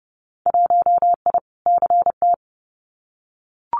Morse code, heard as a keyed tone, about 30 wpm, 700 Hz. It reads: 1I CT
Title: Winter Field Day Exchanges 30wpm